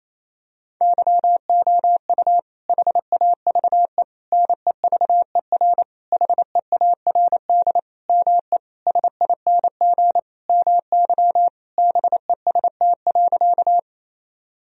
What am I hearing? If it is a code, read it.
Morse code, heard as a keyed tone, about 28 wpm, 700 Hz. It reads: YOU HAVE NEVER HEARD ME SING MY BEST.
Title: Sentences from Top 1000 Words - Review Part 2 - 28wpm